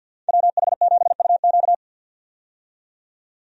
Morse code, heard as a keyed tone, about 51 wpm, 700 Hz. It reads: WH7UQ
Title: USA Call Signs - 5 Characters - 50wpm